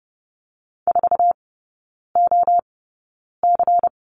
Morse code, heard as a keyed tone, about 30 wpm, spaced as for 9 wpm, 700 Hz. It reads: 4OC
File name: Lesson 16e - Letter c - Three Characters 30wpm